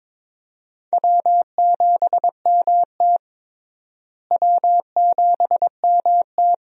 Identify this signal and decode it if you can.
Morse code, heard as a keyed tone, about 22 wpm, 700 Hz. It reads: W7MT W7MT
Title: USA Call Signs - 4 Characters - Repeated 2x - 22wpm